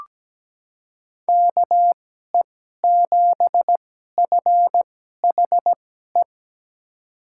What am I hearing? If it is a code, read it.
Morse code, heard as a keyed tone, about 17 wpm, spaced as for 12 wpm, 700 Hz. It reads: KE7FHE
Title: USA Call Signs - 2x Character Spacing - 17wpm